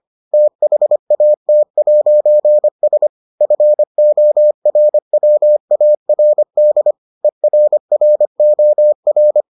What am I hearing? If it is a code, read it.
Morse code, heard as a keyed tone, about 25 wpm, 600 Hz. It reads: THAT'S FORWARD ERROR